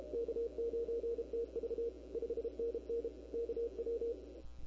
Morse code, heard as a keyed tone, about 34 wpm, 470 Hz. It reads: K9TV 5NN KW
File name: CW / 2005-k5tr arrl cw dx -23:00